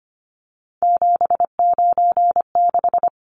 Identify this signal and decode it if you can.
Morse code, heard as a keyed tone, about 25 wpm, 690 Hz. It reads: 796
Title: Numbers - 3-Digit 25wpm